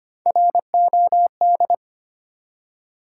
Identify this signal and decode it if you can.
Morse code, heard as a keyed tone, about 25 wpm, 700 Hz. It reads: ROD
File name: practice / 25wpm / Easy Words with Hints - 25wpm